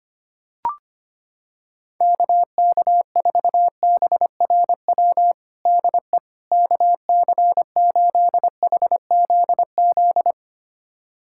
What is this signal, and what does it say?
Morse code, heard as a keyed tone, about 25 wpm, 700 Hz. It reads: KK4BRW DE KC8HZZ
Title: USA Callsign DE Callsign - 6 Characters - 25wpm